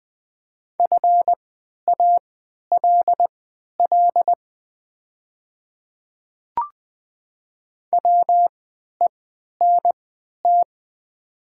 Morse code, heard as a keyed tone, about 20 wpm, spaced as for 11 wpm, 700 Hz. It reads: FALL WENT
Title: 3x Character Spacing - Top 500 Words - 20wpm